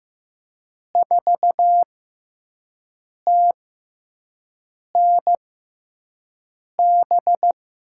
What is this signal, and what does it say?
Morse code, heard as a keyed tone, about 15 wpm, spaced as for 5 wpm, 700 Hz. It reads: 4TNB
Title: USA Call Signs - 6 Characters - 6x Character Spacing - 15wpm